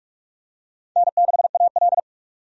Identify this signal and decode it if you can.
Morse code, heard as a keyed tone, about 46 wpm, 700 Hz. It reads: N6AL
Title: USA Call Signs - 4 Characters - 45wpm